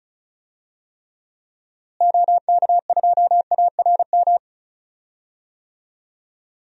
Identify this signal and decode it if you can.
Morse code, heard as a keyed tone, about 35 wpm, 700 Hz. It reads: OK2ARM
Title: DXCC-100 Call Signs - 6 Characters - Repeated 3x - 35wpm